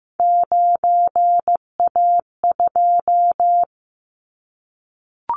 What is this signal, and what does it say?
Morse code, heard as a keyed tone, about 15 wpm, 700 Hz. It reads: 9A2
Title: Callsign Prefixes for Top 25 DXCC - Repeated 3x - 15wpm